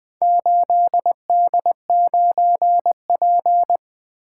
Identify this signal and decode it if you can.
Morse code, heard as a keyed tone, about 20 wpm, 700 Hz. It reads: 8D9P
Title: Four Letter-Number Combinations - 20wpm